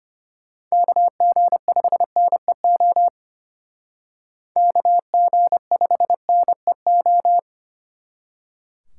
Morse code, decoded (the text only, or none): KG5NEO KG5NEO